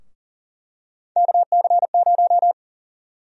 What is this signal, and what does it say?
Morse code, heard as a keyed tone, about 40 wpm, 700 Hz. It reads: KC0
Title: USA Call Sign Prefixes - Repeated 2x - 40wpm